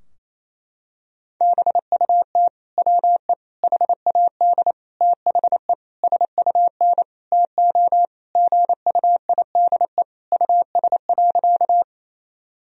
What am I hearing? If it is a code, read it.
Morse code, heard as a keyed tone, about 28 wpm, 700 Hz. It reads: BUT WE HAD THE SUN TO GUIDE US.
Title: Sentences from Top 1000 Words - Review Part 3 - 28wpm